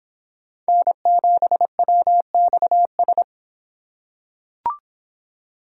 Morse code, heard as a keyed tone, about 25 wpm, 700 Hz. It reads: N7WXS